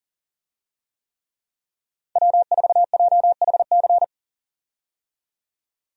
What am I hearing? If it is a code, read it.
Morse code, heard as a keyed tone, about 40 wpm, 700 Hz. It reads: W4JHC
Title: USA Call Signs - 5 Characters - 40wpm